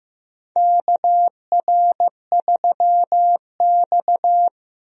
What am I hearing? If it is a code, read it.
Morse code, heard as a keyed tone, about 15 wpm, 700 Hz. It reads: KR3X
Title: USA Call Signs - 4 Characters - Repeated 2x - 15wpm